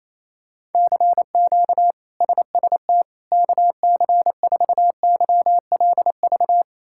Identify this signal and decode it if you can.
Morse code, heard as a keyed tone, about 28 wpm, 700 Hz. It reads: CQ SST KC4YLV
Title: K1USN SST QSO Practice - 28wpm